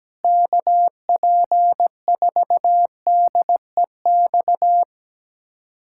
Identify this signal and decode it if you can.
Morse code, heard as a keyed tone, about 17 wpm, 700 Hz. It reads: KP4DEX
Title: USA Callsign DE Callsign - 6 Characters - 17wpm